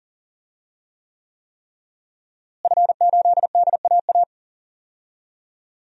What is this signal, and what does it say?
Morse code, heard as a keyed tone, about 37 wpm, 700 Hz. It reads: F8DAA
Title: DXCC-100 Call Signs - 5 Characters - Repeated 3x - 40wpm